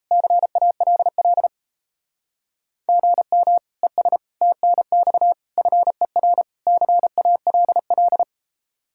Morse code, heard as a keyed tone, about 34 wpm, 700 Hz. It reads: CALL GM ES TNX FER CALL
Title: Speed-Racing Common QSO Phrases - 28wpm